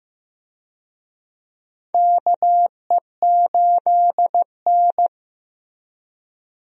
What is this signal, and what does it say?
Morse code, heard as a keyed tone, about 15 wpm, 700 Hz. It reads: KE8N
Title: USA Call Signs - 4 Characters - Repeated 3x - 15wpm